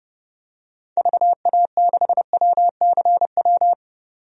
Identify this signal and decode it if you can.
Morse code, heard as a keyed tone, about 32 wpm, 700 Hz. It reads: VA6WCW
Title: DXCC-100 Call Signs - 6 Characters - 30wpm